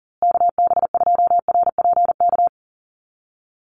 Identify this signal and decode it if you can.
Morse code, heard as a keyed tone, about 40 wpm, 700 Hz. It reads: KB2RPK
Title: USA Call Signs Repeated 2x 40wpm